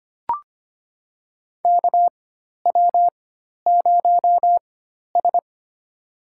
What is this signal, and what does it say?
Morse code, heard as a keyed tone, about 25 wpm, spaced as for 12 wpm, 700 Hz. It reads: KW0S
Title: USA Call Signs - 4x Character Spacing - 25wpm